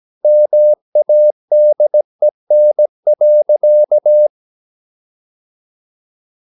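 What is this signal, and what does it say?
Morse code, heard as a keyed tone, about 17 wpm, 600 Hz. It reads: MADEN.
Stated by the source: Slow (QRS) Morse practice at 17 wpm, from CW med Gnister.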